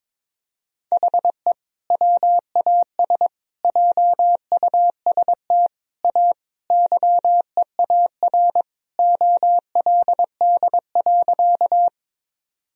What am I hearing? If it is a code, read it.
Morse code, heard as a keyed tone, about 22 wpm, 700 Hz. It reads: HE WAS JUST A YEAR OLD.
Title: Sentences from Top 1000 Words - Review Part 1 - 22wpm